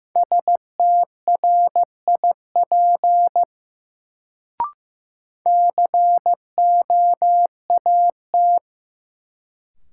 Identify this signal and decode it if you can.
Morse code, heard as a keyed tone, about 16 wpm, 700 Hz. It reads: STRIP COAT
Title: Sentences from Top 1600 Words - New Word Drill Alt - 15wpm